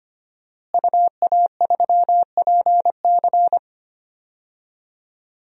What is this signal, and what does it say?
Morse code, heard as a keyed tone, about 25 wpm, 700 Hz. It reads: UA3PC
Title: DXCC-100 Call Signs - 5 Characters - Repeated 3x - 25wpm